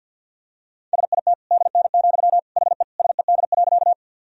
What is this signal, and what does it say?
Morse code, heard as a keyed tone, about 52 wpm, 700 Hz. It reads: SIT DN, HE SED.
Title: Sentences from Top 500 Words and Abbreviations 50wpm